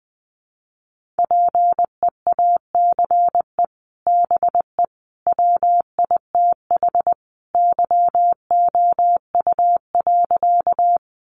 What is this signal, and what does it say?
Morse code, heard as a keyed tone, about 20 wpm, 700 Hz. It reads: PEACE BE WITH YOU.
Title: Sentences from Top 1500 Words 20wpm